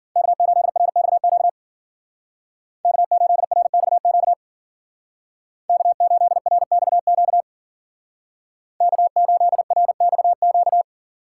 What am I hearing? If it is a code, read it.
Morse code, heard as a keyed tone, about 49 wpm, 700 Hz. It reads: K8RXQ K8RXQ K8RXQ K8RXQ
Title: Speed Racing - USA 1x3 Call Signs 40wpm